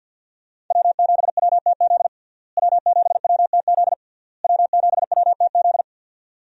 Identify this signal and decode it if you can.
Morse code, heard as a keyed tone, about 51 wpm, 700 Hz. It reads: W7WTZ W7WTZ W7WTZ
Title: USA Call Signs - 5 Characters - Repeated 3x - 50wpm